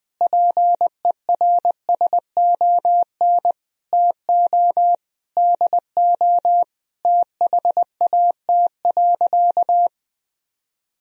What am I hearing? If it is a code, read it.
Morse code, heard as a keyed tone, about 20 wpm, 700 Hz. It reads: PERSON TO DO THAT.
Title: Sentences from Top 1000 Words - Review Part 3 - 20wpm